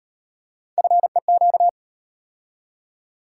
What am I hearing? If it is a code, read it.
Morse code, heard as a keyed tone, about 32 wpm, 700 Hz. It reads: FEQ